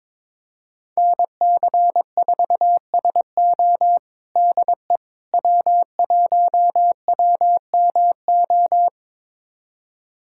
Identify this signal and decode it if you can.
Morse code, heard as a keyed tone, about 22 wpm, 700 Hz. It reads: NC4SO DE W1WMO
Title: USA Callsign DE Callsign - 5 Characters - 22wpm